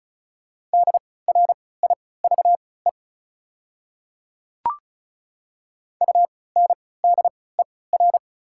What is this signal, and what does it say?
Morse code, heard as a keyed tone, about 36 wpm, spaced as for 20 wpm, 690 Hz. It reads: DRIVE UNDER
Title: 3x Character Spacing - Top 500 Words - 35wpm